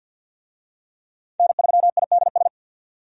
Morse code, heard as a keyed tone, about 52 wpm, 700 Hz. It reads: N3IDS
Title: USA Call Signs - 5 Characters - 50wpm